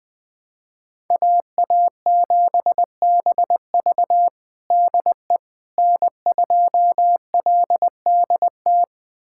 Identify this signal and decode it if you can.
Morse code, heard as a keyed tone, about 20 wpm, 700 Hz. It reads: AA7BV DE N2LDT
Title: USA Callsign DE Callsign - 5 Characters - 20wpm